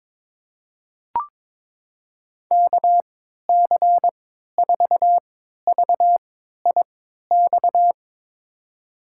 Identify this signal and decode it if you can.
Morse code, heard as a keyed tone, about 22 wpm, spaced as for 13 wpm, 700 Hz. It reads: KC4VIX
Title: USA Call Signs - 6 Characters - 3x Character Spacing - 22wpm